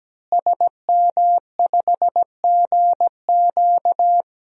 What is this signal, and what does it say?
Morse code, heard as a keyed tone, about 17 wpm, 700 Hz. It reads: SM5GQ
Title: DXCC-100 Call Signs - 5 Characters - Repeated 3x - 17wpm